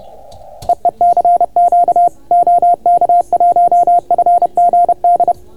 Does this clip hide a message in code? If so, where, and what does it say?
Yes: Morse code from 0.7 to 5.3 s, heard as a keyed tone, about 30 wpm, 690 Hz. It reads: EE CQ OK1FGD